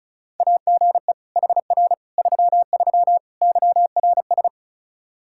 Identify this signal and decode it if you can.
Morse code, heard as a keyed tone, about 36 wpm, 700 Hz. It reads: AGE HR 33 YRS
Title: K1USN SST QSO Practice 35wpm